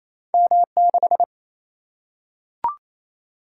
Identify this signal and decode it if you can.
Morse code, heard as a keyed tone, about 28 wpm, 700 Hz. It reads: M6
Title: Single-Letter Single-Number 28wpm